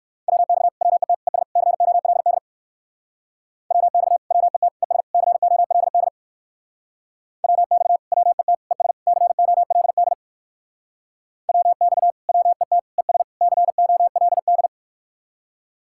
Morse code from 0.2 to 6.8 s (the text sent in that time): WX WET ES COLD WX WET ES COLD